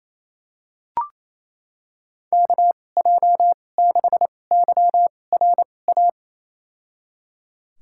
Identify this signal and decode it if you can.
Morse code, heard as a keyed tone, about 28 wpm, spaced as for 20 wpm, 700 Hz. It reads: KJ6YRA
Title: USA Call Signs - 6 Characters - 2x Character Spacing - 28wpm